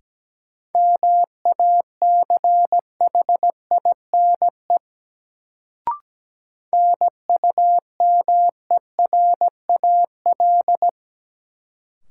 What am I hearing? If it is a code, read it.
Morse code, heard as a keyed tone, about 17 wpm, 700 Hz. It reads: MACHINE NUMERAL
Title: Sentences from Top 1000 Words - Review Part 2 - New Word Drill - 17wpm